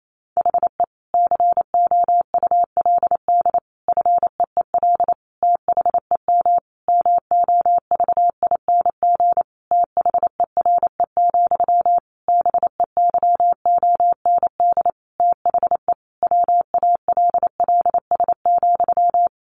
Morse code, heard as a keyed tone, about 28 wpm, 700 Hz. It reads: HE COULD FEEL THEM MOVING THERE, BEYOND THE WALLS,
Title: A Sound Of Thunder 28wpm